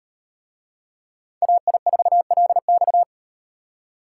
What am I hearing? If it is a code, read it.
Morse code, heard as a keyed tone, about 38 wpm, 700 Hz. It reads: AI4LX